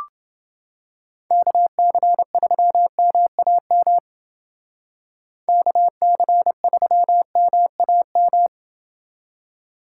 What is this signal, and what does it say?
Morse code, heard as a keyed tone, about 29 wpm, 700 Hz. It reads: KC3MAM KC3MAM